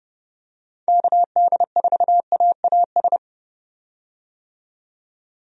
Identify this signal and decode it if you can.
Morse code, heard as a keyed tone, about 30 wpm, 700 Hz. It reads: KD4AAS
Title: USA Call Signs - 6 Characters - 30wpm